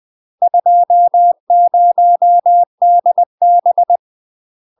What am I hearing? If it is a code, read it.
Morse code, heard as a keyed tone, about 20 wpm, 700 Hz. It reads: 20DB
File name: UK MASTER QSOs TRAINING / 20WPM / UK MASTER QSOs 0002